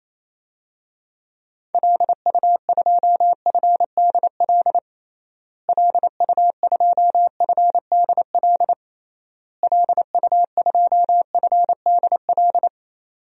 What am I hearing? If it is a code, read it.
Morse code, heard as a keyed tone, about 28 wpm, 700 Hz. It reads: LU2FDL LU2FDL LU2FDL
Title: DXCC-100 Call Signs - 6 Characters - Repeated 3x - 28wpm